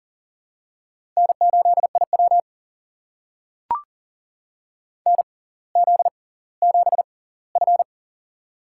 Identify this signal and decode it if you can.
Morse code, heard as a keyed tone, about 40 wpm, spaced as for 20 wpm, 700 Hz. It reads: N8IW NZ7F
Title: USA Call Signs - 4 Characters - 6x Character Spacing - 40wpm